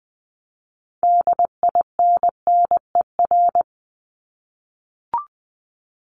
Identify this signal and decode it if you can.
Morse code, heard as a keyed tone, about 20 wpm, 700 Hz. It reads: DINNER
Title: Top 1751-2000 Words 20wpm